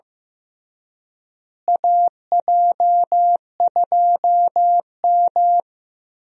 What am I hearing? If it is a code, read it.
Morse code, heard as a keyed tone, about 15 wpm, 700 Hz. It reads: AJ2M
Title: USA Call Signs - 4 Characters - Repeated 3x - 15wpm